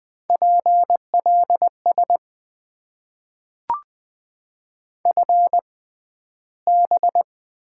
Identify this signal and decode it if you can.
Morse code, heard as a keyed tone, about 20 wpm, spaced as for 12 wpm, 700 Hz. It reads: PLS FB
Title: Top 90 QSO Elements - 6x Character Spacing - 20wpm